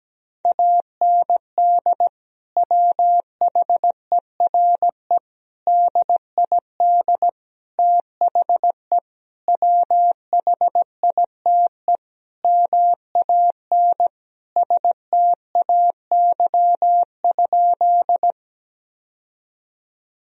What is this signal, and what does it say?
Morse code, heard as a keyed tone, about 17 wpm, 700 Hz. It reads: AND WHERE DID THE WHITE MAN STAY?
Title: Sentences from Top 1000 Words - Review Part 1 - 17wpm